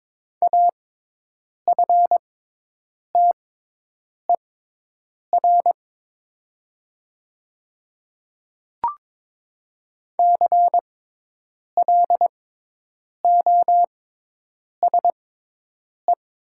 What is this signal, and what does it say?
Morse code, heard as a keyed tone, about 22 wpm, spaced as for 8 wpm, 700 Hz. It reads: AFTER CLOSE
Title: 6x Character Spacing - Top 500 Words - 22wpm